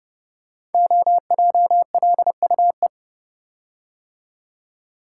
Morse code, heard as a keyed tone, about 32 wpm, 700 Hz. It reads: OJLUE
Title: Code Groups - 5 Letter:Numbers - 30wpm